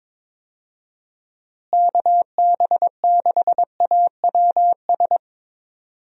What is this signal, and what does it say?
Morse code, heard as a keyed tone, about 22 wpm, 700 Hz. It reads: KB6AWS
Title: USA Call Signs - 6 Characters - Repeated 3x - 22wpm